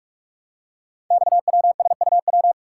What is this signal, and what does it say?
Morse code, heard as a keyed tone, about 44 wpm, 700 Hz. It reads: XWSUW
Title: Code Groups - 5 Letters - 45wpm